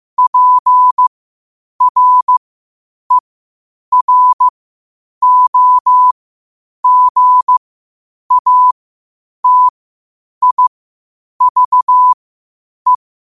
Morse code, heard as a keyed tone, about 15 wpm, spaced as for 8 wpm, 1000 Hz. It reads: PREROGATIVE
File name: qotdfiles / 8wpm